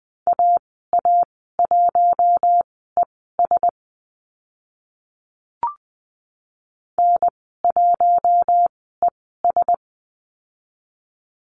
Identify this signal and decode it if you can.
Morse code, heard as a keyed tone, about 20 wpm, spaced as for 14 wpm, 700 Hz. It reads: AA1ES N1ES
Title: 062 Lesson 11g - Letter h - Call Signs 20wpm